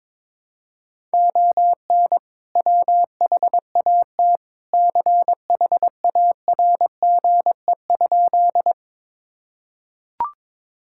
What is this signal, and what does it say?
Morse code, heard as a keyed tone, about 22 wpm, 700 Hz. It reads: ON WHAT CHARGE?
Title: Sentences from Top 1000 Words - Review Part 6 - 22wpm